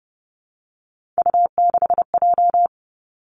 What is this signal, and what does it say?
Morse code, heard as a keyed tone, about 30 wpm, 700 Hz. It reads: U6J
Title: Three Challenging Letter-Number Combinations 30wpm